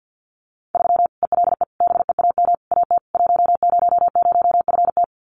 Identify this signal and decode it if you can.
Morse code, heard as a keyed tone, about 48 wpm, 700 Hz. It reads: 3 ELE BEAM AT 100FT
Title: Common QSO Phrases 50wpm